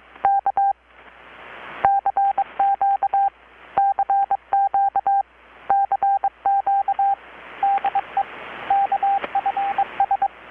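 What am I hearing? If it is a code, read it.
Morse code, heard as a keyed tone, about 23 wpm, 790 Hz, with radio noise. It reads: K CQ CQ CQ DE KFS
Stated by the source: Morse from KPH's Night of Nights XX maritime radiotelegraphy event, received in Chiapas, Mexico about 3,800 km away on a Kenwood R-600 with a grounded 10 m long-wire antenna.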